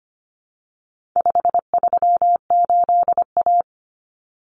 Morse code, heard as a keyed tone, about 25 wpm, 700 Hz. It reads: 538A
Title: Common Cut Numbers - 4-Digit 25wpm